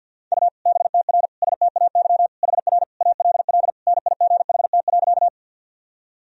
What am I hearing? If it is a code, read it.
Morse code, heard as a keyed tone, about 52 wpm, 700 Hz. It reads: U BTR STAY HR ALL NIGHT.
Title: Sentences from Top 500 Words and Abbreviations 50wpm